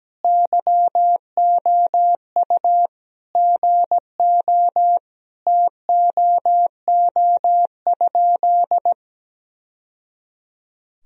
Morse code, heard as a keyed tone, about 17 wpm, 700 Hz. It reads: YOU GO TOO?
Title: Sentences from Top 1000 Words - Review Part 2 - 17wpm